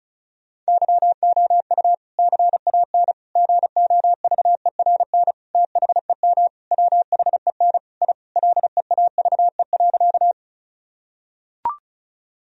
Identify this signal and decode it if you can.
Morse code, heard as a keyed tone, about 34 wpm, 700 Hz. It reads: YOU CAN GOVERN THEM WHEN I LEAVE.
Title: Sentences from Top 1000 Words - Review Part 2 - 35wpm